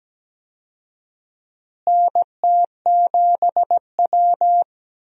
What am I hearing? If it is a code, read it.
Morse code, heard as a keyed tone, about 17 wpm, 700 Hz. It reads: NT7W
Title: USA Call Signs - 4 Characters - Repeated 2x - 17wpm